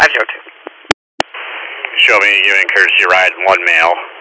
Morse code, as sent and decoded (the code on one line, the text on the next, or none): none